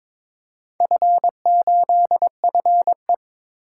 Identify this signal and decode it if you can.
Morse code, heard as a keyed tone, about 22 wpm, 700 Hz. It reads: F8FE
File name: practice / 22wpm / Four Letter-Number Combinations - 22wpm